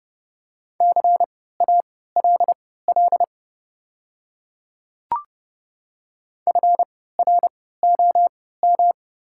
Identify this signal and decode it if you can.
Morse code, heard as a keyed tone, about 30 wpm, spaced as for 17 wpm, 700 Hz. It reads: CALL FROM
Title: 3x Character Spacing - Top 200 Words - 30wpm